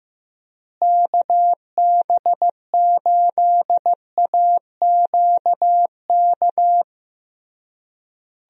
Morse code, heard as a keyed tone, about 15 wpm, 700 Hz. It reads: KB8AQK
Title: USA Call Signs - 6 Characters - Repeated 3x - 15wpm